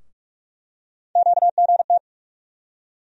Morse code, heard as a keyed tone, about 46 wpm, 700 Hz. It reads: QGT
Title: USA Call Sign Postfixes - 3 Letters 45wpm